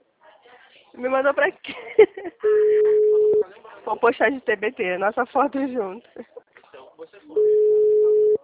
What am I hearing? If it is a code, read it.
Ringback tone (Europe, 425 Hz), with speech.